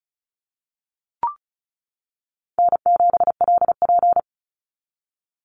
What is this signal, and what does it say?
Morse code, heard as a keyed tone, about 34 wpm, 700 Hz. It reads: N7LP
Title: NA SOTA Activator Callsigns 35wpm